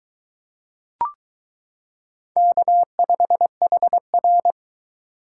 Morse code, heard as a keyed tone, about 22 wpm, 700 Hz. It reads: K5HR